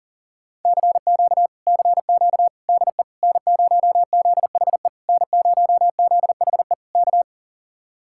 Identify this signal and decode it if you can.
Morse code, heard as a keyed tone, about 49 wpm, 700 Hz. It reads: CQ CQ DE N0ZHE N0ZHE K
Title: Speed-Racing Common QSO Phrases - 40wpm